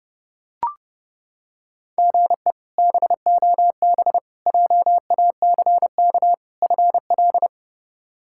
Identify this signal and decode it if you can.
Morse code, heard as a keyed tone, about 30 wpm, 700 Hz. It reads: GE BOB JACK FL
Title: K1USN SST QSO Practice - 30wpm